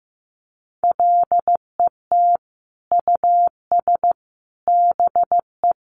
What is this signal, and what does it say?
Morse code, heard as a keyed tone, about 15 wpm, 700 Hz. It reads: LET US BE
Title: Sentences from Top 1300 Words 15wpm